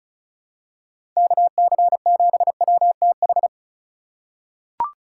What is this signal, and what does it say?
Morse code, heard as a keyed tone, about 34 wpm, 700 Hz. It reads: KC7WTH
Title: USA Call Signs - 6 Characters - 2x Character Spacing - 35wpm